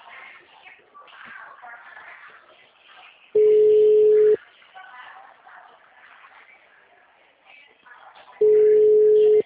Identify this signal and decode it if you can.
Ringback tone (Europe, 425 Hz), with speech.
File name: out- -2011- - - .94525